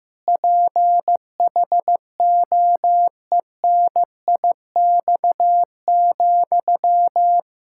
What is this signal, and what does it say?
Morse code, heard as a keyed tone, about 15 wpm, 700 Hz. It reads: PHOENIX,
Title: Practice QSO - Exchange 1 - Beginning - 15wpm